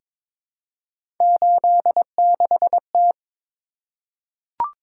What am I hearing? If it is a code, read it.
Morse code, heard as a keyed tone, about 22 wpm, 700 Hz. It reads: 86T
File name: practice / 22wpm / Three Letter-Number Combinations - 22wpm